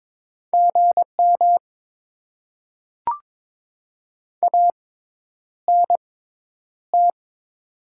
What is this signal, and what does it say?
Morse code, heard as a keyed tone, about 22 wpm, spaced as for 10 wpm, 700 Hz. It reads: GM ANT
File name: Top 20 QSO Elements - 6x Character Spacing - 22wpm